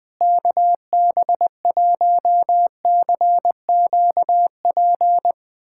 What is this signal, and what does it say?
Morse code, heard as a keyed tone, about 21 wpm, 700 Hz. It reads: KB1CQP
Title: USA Callsign DE Callsign 20wpm